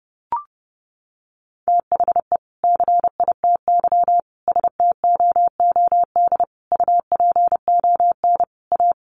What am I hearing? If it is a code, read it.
Morse code, heard as a keyed tone, about 30 wpm, 700 Hz. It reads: THE CITY STOOD UPON A
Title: Sentences from Top 1100 Words 30wpm